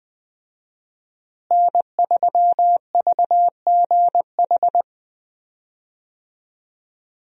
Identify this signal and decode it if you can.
Morse code, heard as a keyed tone, about 20 wpm, 700 Hz. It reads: N3VGH
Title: USA Call Signs - 5 Characters - Repeated 3x - 20wpm